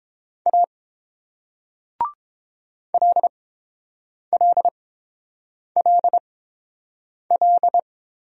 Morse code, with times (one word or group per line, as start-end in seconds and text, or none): 0.5-0.6 A
2.9-7.8 LLLL